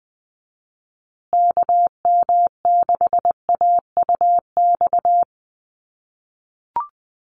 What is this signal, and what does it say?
Morse code, heard as a keyed tone, about 20 wpm, 700 Hz. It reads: KM6AUX
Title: USA Call Signs Repeated 2x 20wpm